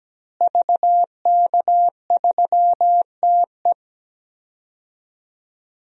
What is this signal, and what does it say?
Morse code, heard as a keyed tone, about 17 wpm, 700 Hz. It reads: VK3TE
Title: DXCC-100 Call Signs - 5 Characters - Repeated 3x - 17wpm